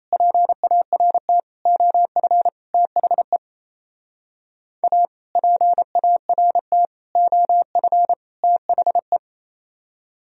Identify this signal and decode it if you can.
Morse code, heard as a keyed tone, about 37 wpm, 700 Hz. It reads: PART OF THE A PART OF THE